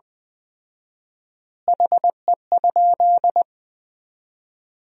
Morse code, heard as a keyed tone, about 20 wpm, 700 Hz. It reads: HE?
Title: 4x Word-Spacing - Sentences from Top 100 Words - 20wpm